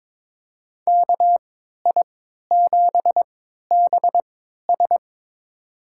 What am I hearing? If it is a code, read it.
Morse code, heard as a keyed tone, about 21 wpm, spaced as for 12 wpm, 700 Hz. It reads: KI7BS
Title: USA Call Signs - 5 Characters - 3x Character Spacing - 22wpm